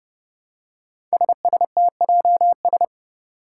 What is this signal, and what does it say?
Morse code, heard as a keyed tone, about 30 wpm, 700 Hz. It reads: SSTJS
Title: Five Challenging Letter-Number Combinations 30wpm